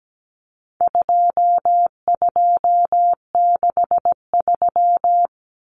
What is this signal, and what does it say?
Morse code, heard as a keyed tone, about 17 wpm, 700 Hz. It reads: 2263
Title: Common Cut Numbers - 4-Digit 17wpm